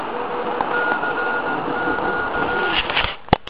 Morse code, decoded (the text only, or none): CQ